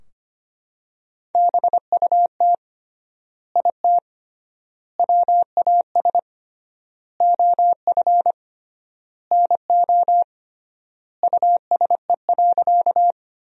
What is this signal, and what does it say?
Morse code, heard as a keyed tone, about 25 wpm, 700 Hz. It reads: BUT IT WAS OF NO USE.
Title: 3x Word-Spacing - Sentences from Top 100 Words - 25wpm